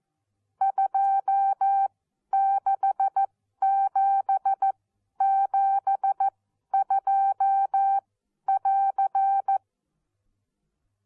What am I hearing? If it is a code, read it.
Morse code, heard as a keyed tone, about 14 wpm, spaced as for 11 wpm, 780 Hz. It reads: 26772<AR>
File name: Onlinelehrgang / morsen / DJ4UF Morsen14